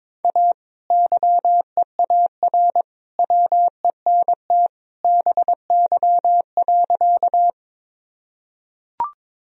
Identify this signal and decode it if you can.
Morse code, heard as a keyed tone, about 22 wpm, 700 Hz. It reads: A YEAR WENT BY.
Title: Sentences from Top 1000 Words - Review Part 1 - 22wpm